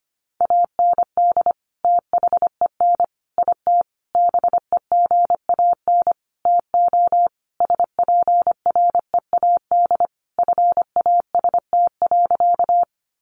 Morse code, heard as a keyed tone, about 25 wpm, 700 Hz. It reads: AND THEN IT BEGAN TO SPREAD FAST.
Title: Sentences from Top 1200 Words 25wpm